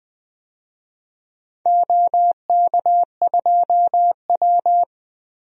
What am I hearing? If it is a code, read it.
Morse code, heard as a keyed tone, about 21 wpm, 700 Hz. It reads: OK2W
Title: DXCC-100 Call Signs - 4 Characters - 20wpm